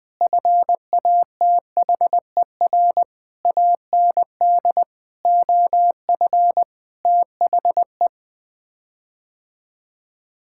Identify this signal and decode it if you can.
Morse code, heard as a keyed tone, about 20 wpm, 700 Hz. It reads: FATHER AND OF THE
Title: Sets of 9 Words - 20wpm